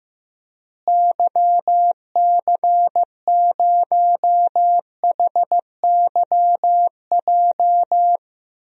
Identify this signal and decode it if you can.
Morse code, heard as a keyed tone, about 15 wpm, 700 Hz. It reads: YC0HYJ
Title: DXCC-100 Call Signs - 6 Characters - Repeated 2x - 15wpm